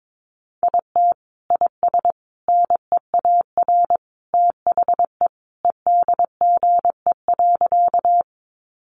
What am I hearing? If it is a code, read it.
Morse code, heard as a keyed tone, about 22 wpm, 700 Hz. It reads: IT IS NEAR THE EDGE.
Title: Additional Easy Phrases 22wpm